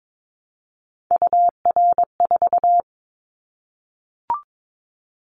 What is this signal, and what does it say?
Morse code, heard as a keyed tone, about 22 wpm, 700 Hz. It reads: UR4
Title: Callsign Prefixes for Top 25 DXCC - 22wpm